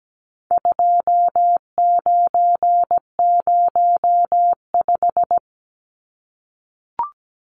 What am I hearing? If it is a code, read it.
Morse code, heard as a keyed tone, about 17 wpm, 700 Hz. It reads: 2905
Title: Numbers - 4-Digit 17wpm